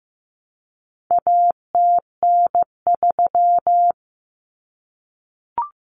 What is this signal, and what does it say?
Morse code, heard as a keyed tone, about 15 wpm, 700 Hz. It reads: ATN3
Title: Common Cut Numbers - 4-Digit 15wpm